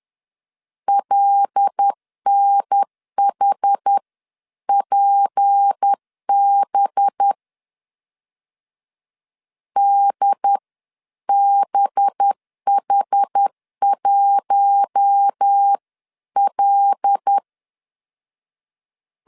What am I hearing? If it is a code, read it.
Morse code, heard as a keyed tone, about 10 wpm, spaced as for 9 wpm, 800 Hz. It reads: LNHPB DBH1L